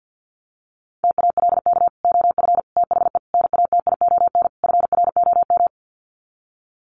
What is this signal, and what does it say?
Morse code, heard as a keyed tone, about 48 wpm, 700 Hz. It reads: TALK OF THE NATION FROM
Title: Sets of 5 Words 50wpm